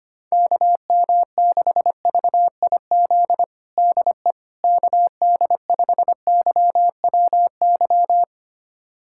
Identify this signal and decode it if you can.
Morse code, heard as a keyed tone, about 25 wpm, 700 Hz. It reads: KM6VIZ DE KD5YWY
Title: USA Callsign DE Callsign - 6 Characters - 25wpm